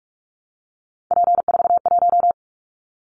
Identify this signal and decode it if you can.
Morse code, heard as a keyed tone, about 46 wpm, 700 Hz. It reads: P41
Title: Callsign Prefixes for Top 100 DXCC - 45wpm